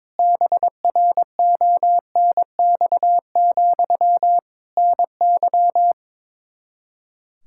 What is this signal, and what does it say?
Morse code, heard as a keyed tone, about 22 wpm, 700 Hz. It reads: BRONX, NY
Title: Practice QSO - Exchange 1 - Beginning - 22wpm